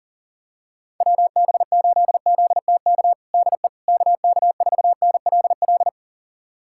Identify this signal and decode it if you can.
Morse code, heard as a keyed tone, about 40 wpm, 700 Hz. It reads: WB8ZTK DE KK4NLL
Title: USA Callsign DE Callsign - 6 Characters - 40wpm